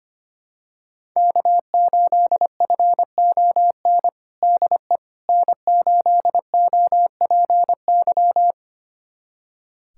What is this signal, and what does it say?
Morse code, heard as a keyed tone, about 25 wpm, 700 Hz. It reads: K8FON DE N8OPY
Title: USA Callsign DE Callsign - 5 Characters - 25wpm